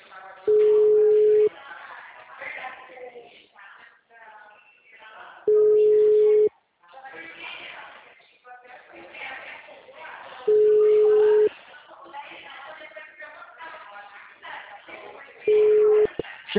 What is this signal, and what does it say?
Ringback tone (Europe, 425 Hz), with speech.